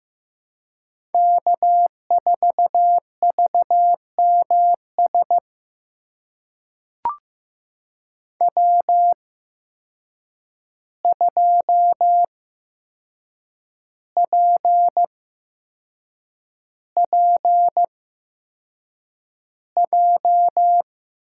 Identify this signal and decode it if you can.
Morse code, heard as a keyed tone, about 16 wpm, spaced as for 6 wpm, 700 Hz. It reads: K4VMS W2PPJ
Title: USA Call Signs - 5 Characters - 8x Character Spacing - 15wpm